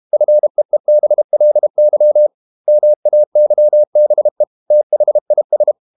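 Morse code, heard as a keyed tone, about 32 wpm, 600 Hz. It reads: FEEBLY MAYBE THIS
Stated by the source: Morse code practice at 32 wpm from CW med Gnister, reading Ken Follett's novel Never in English.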